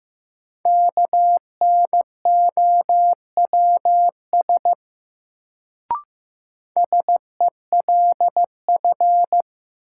Morse code, heard as a keyed tone, about 15 wpm, 700 Hz. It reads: KNOWS SELF
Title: Sentences from Top 1000 Words - Review Part 1 - New Word Drill - 15wpm